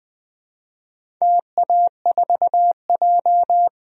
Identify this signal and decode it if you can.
Morse code, heard as a keyed tone, about 20 wpm, 700 Hz. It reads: TA4J
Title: DXCC-100 Call Signs - 4 Characters - 20wpm